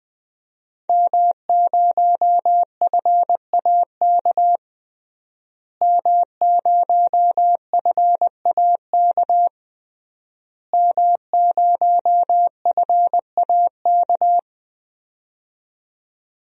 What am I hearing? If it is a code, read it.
Morse code, heard as a keyed tone, about 21 wpm, 700 Hz. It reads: M0FAK M0FAK M0FAK
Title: DXCC-100 Call Signs - 5 Characters - Repeated 3x - 20wpm